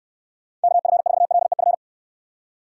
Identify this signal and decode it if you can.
Morse code, heard as a keyed tone, about 69 wpm, 700 Hz. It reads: KF4LEF